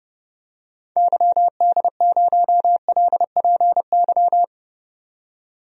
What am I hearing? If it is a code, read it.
Morse code, heard as a keyed tone, about 30 wpm, 700 Hz. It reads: YD0LPY
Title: DXCC-100 Call Signs - 6 Characters - 30wpm